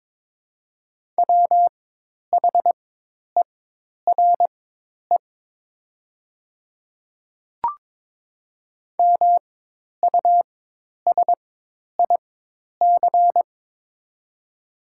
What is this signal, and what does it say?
Morse code, heard as a keyed tone, about 22 wpm, spaced as for 10 wpm, 700 Hz. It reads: WHERE MUSIC
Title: 4x Character Spacing - Top 300 Words - 22wpm